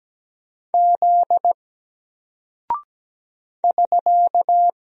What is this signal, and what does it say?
Morse code, heard as a keyed tone, about 17 wpm, 700 Hz. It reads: Z <SK>
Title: Single Character (Letter:Number:Punctuation:Prosign) 17wpm